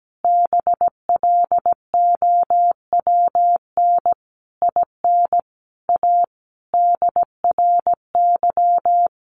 Morse code, heard as a keyed tone, about 17 wpm, 700 Hz. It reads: BLOWN IN A DRY
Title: A Sound Of Thunder 17wpm